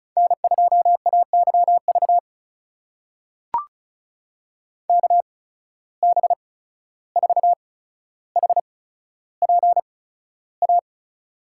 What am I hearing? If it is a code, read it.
Morse code, heard as a keyed tone, about 34 wpm, spaced as for 14 wpm, 700 Hz. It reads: N2AYV KB4HPA
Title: USA Call Signs - 8x Character Spacing - 35wpm